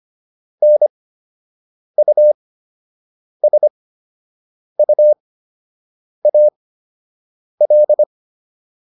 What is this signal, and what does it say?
Morse code, heard as a keyed tone, about 25 wpm, spaced as for 7 wpm, 600 Hz. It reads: NUSUAL